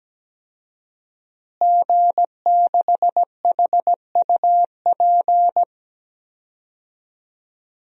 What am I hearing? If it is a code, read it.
Morse code, heard as a keyed tone, about 17 wpm, 700 Hz. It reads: G6HUP
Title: DXCC-100 Call Signs - 5 Characters - Repeated 2x - 17wpm